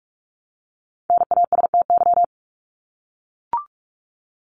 Morse code, heard as a keyed tone, about 44 wpm, 700 Hz. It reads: NASTY
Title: Top 1751-2000 Words 45wpm